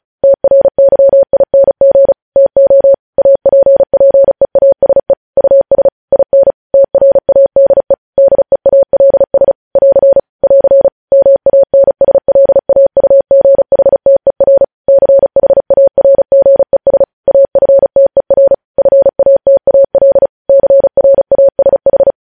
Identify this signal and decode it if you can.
Morse code, heard as a keyed tone, about 36 wpm, 550 Hz. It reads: TRYING TO APPEASE US IN TRADE DEALS <AR> <AR> MANSLAUGHTER CHARGES AFTER FATAL CRASH
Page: News Headlines in Morse Code at 35 WPM